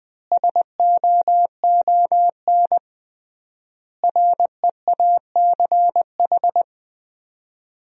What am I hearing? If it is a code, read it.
Morse code, heard as a keyed tone, about 20 wpm, 700 Hz. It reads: SOON REACH
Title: 3x Word-Spacing - Sentences from Top 400 Words - 20wpm